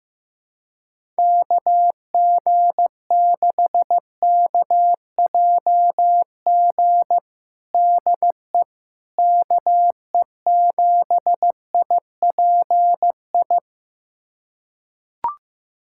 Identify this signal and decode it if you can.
Morse code, heard as a keyed tone, about 15 wpm, 700 Hz. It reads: KG6KJG DE KE7IPI
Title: USA Callsign DE Callsign - 6 Characters - 15wpm